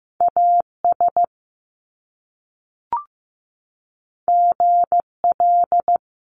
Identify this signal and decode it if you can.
Morse code, heard as a keyed tone, about 15 wpm, 700 Hz. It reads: AS GL
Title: Top Two Letter Combinations in Words 15wpm